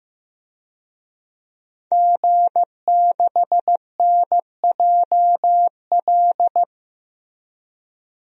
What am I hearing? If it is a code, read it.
Morse code, heard as a keyed tone, about 15 wpm, 700 Hz. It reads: G6NJL
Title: DXCC-100 Call Signs - 5 Characters - 15wpm